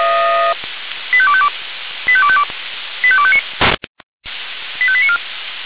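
Five-tone selective call (ZVEI), heard as quick runs of five tones, tones of about 70 ms. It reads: 85252 85252 85259 85894
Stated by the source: ZVEI five-tone call 85310 received by SDR, audio.f85360.feuerwehr-orlamuende.org fire brigade server.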